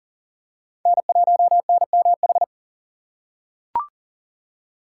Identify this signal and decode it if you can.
Morse code, heard as a keyed tone, about 40 wpm, 700 Hz. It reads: N1NMH
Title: USA Call Signs - 5 Characters - 4x Character Spacing - 40wpm